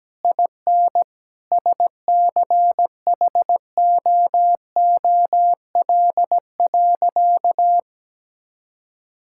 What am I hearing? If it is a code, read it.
Morse code, heard as a keyed tone, about 17 wpm, 700 Hz. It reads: IN SCHOOL.
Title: Sentences from Top 1000 Words - Review Part 5 - 17wpm